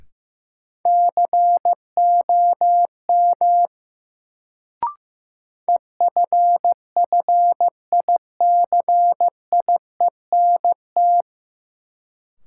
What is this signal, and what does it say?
Morse code, heard as a keyed tone, about 16 wpm, 700 Hz. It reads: COM EFFICIENT
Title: Sentences from Top 1300 Words - New Word Drill Alt - 15wpm